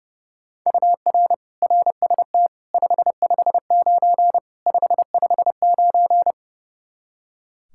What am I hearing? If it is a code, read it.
Morse code, heard as a keyed tone, about 30 wpm, 700 Hz. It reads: UR RST 559 559
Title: Practice QSO - Exchange 1 - Beginning - 30wpm